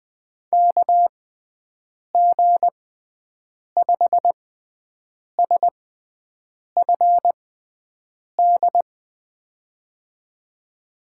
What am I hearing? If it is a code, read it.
Morse code, heard as a keyed tone, about 20 wpm, spaced as for 7 wpm, 700 Hz. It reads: KG5SFD
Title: USA Call Signs - 6 Characters - 6x Character Spacing - 20wpm